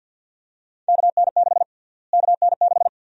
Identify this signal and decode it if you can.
Morse code, heard as a keyed tone, about 51 wpm, 700 Hz. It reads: KN6 KN6
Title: USA Call Sign Prefixes - Repeated 2x - 50wpm